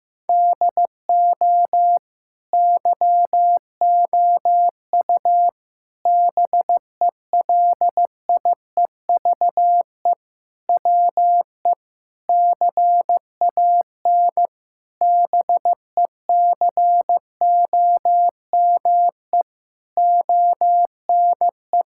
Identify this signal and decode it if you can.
Morse code, heard as a keyed tone, about 15 wpm, 700 Hz. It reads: DO YOU BELIEVE WE CAN BECOME ONE
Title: Sentences from Top 1000 Words - Review Part 6 - 15wpm